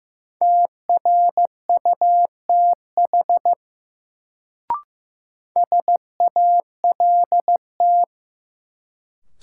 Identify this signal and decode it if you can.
Morse code, heard as a keyed tone, about 15 wpm, 700 Hz. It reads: TRUTH SALT
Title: Sentences from Top 800 Words - New Word Drill - 15wpm